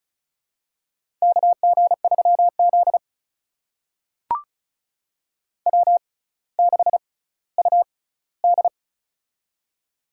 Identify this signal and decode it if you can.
Morse code, heard as a keyed tone, about 36 wpm, spaced as for 18 wpm, 700 Hz. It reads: KG3Z W6UD
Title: USA Call Signs - 6x Character Spacing - 35wpm